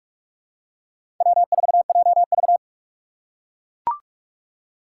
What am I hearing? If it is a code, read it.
Morse code, heard as a keyed tone, about 43 wpm, 700 Hz. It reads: W4JV
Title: USA Call Signs - 4x Character Spacing - 45wpm